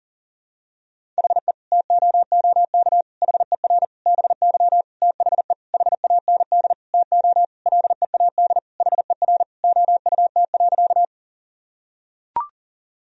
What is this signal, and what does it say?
Morse code, heard as a keyed tone, about 40 wpm, 700 Hz. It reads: HE TOOK HER BY THE HAND TO LEAD HER OUT.
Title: Sentences from Top 1000 Words - Review Part 2 - 40wpm